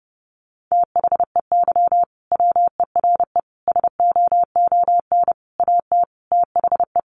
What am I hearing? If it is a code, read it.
Morse code, heard as a keyed tone, about 30 wpm, 700 Hz. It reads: THEY WERE SOON AT THE
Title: Sentences from Top 1200 Words 30wpm